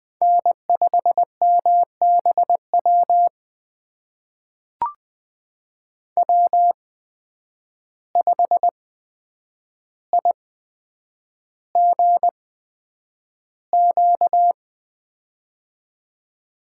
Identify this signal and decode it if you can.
Morse code, heard as a keyed tone, about 20 wpm, spaced as for 9 wpm, 700 Hz. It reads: N5MBW W5IGQ
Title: USA Call Signs - 5 Characters - 8x Character Spacing - 20wpm